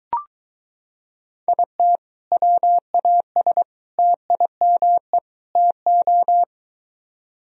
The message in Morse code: IT WAS TIME TO

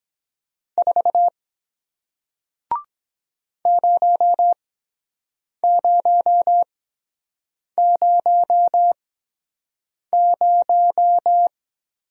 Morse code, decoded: 4 0000

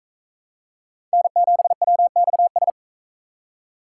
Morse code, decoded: N7WXS

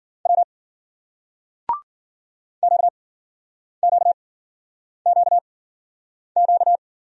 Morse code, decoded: W QQQQ

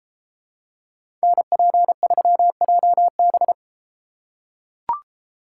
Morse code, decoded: NP3JB

